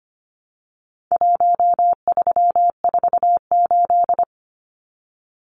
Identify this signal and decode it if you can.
Morse code, heard as a keyed tone, about 25 wpm, 700 Hz. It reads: 1348
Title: Numbers - Contest Style - 4-Digit 25wpm